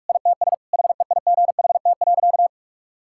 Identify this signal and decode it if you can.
Morse code, heard as a keyed tone, about 42 wpm, 700 Hz. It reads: ITS HEIGHT.
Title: Sentences from Top 1600 Words 45wpm